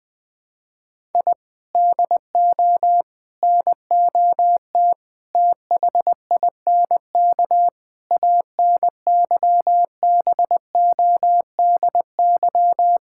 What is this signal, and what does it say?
Morse code, heard as a keyed tone, about 20 wpm, 700 Hz. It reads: I DO NOT THINK ANYBODY
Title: Sentences with 5-Word Phrases - Spoken Beginning - 20wpm